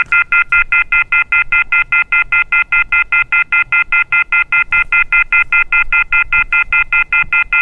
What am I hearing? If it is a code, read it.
Telephone call progress tone, heard as a telephone network's tone, about 1400, 2050, 2450, 2600 Hz, with noise. Off-hook warning tone.